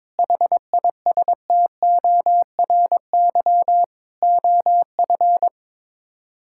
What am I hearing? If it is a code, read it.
Morse code, heard as a keyed tone, about 22 wpm, 700 Hz. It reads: HISTORY OF
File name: practice / 22wpm / Sets of 8 Words - 22wpm